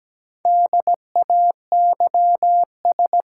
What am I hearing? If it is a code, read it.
Morse code, heard as a keyed tone, about 17 wpm, 700 Hz. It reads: DAYS
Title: Sets of 5 Words Encore - New Word Drill - 17wpm